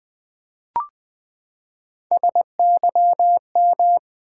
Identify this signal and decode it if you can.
Morse code, heard as a keyed tone, about 20 wpm, 700 Hz. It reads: SYM
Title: Build A Word - 3 Letters At A Time - 8-Letter Words - 20wpm